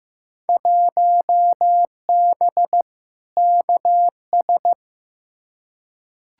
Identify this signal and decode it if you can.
Morse code, heard as a keyed tone, about 15 wpm, 700 Hz. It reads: 1B KS
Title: Field Day Exchanges Encore - 15wpm